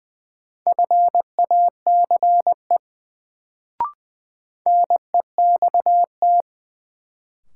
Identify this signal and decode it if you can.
Morse code, heard as a keyed tone, about 20 wpm, 700 Hz. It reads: FACE NEXT
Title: Sentences from Top 300 Words - New Word Drill - 20wpm